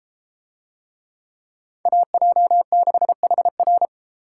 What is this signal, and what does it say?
Morse code, heard as a keyed tone, about 33 wpm, 700 Hz. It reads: AJ6HR